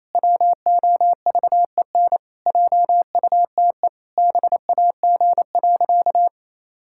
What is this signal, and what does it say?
Morse code, heard as a keyed tone, about 28 wpm, 700 Hz. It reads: WOVEN JUTE BAG.
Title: Pangram Challenge v2 - 28wpm